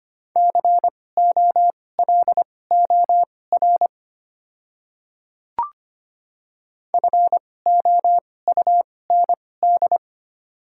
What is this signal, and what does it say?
Morse code, heard as a keyed tone, about 25 wpm, spaced as for 18 wpm, 700 Hz. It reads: COLOR FOUND
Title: 2x Character Spacing - Top 300 Words - 25wpm